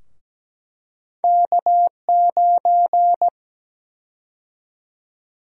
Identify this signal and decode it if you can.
Morse code, heard as a keyed tone, about 17 wpm, 700 Hz. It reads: K9
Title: USA Call Sign Prefixes - Repeated 2x - 17wpm